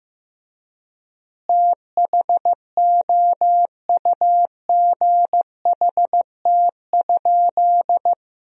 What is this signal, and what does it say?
Morse code, heard as a keyed tone, about 15 wpm, 700 Hz. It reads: THOUGHT?
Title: 4x Word-Spacing - Sentences from Top 300 Words - 15wpm